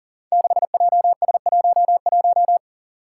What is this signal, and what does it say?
Morse code, heard as a keyed tone, about 40 wpm, 700 Hz. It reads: 6JS11
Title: Five Challenging Letter-Number Combinations 40wpm